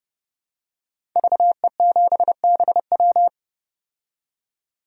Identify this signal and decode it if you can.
Morse code, heard as a keyed tone, about 32 wpm, 700 Hz. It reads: VE7BW
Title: DXCC-100 Call Signs - 5 Characters - 30wpm